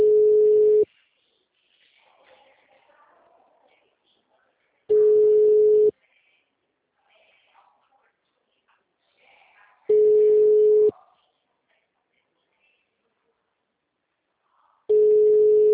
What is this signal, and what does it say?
Ringback tone (Europe, 425 Hz).
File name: out- -2017- - - .2